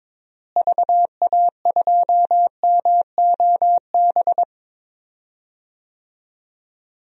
Morse code, heard as a keyed tone, about 21 wpm, 700 Hz. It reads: VA2MOB
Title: DXCC-100 Call Signs - 6 Characters - Repeated 3x - 22wpm